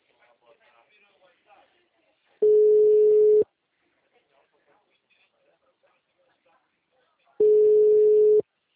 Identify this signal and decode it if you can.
Ringback tone (Europe, 425 Hz), with speech.